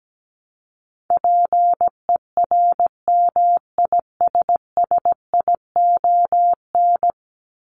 Morse code, heard as a keyed tone, about 17 wpm, 700 Hz. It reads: PERMISSION
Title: Top 1501-1750 Words 17wpm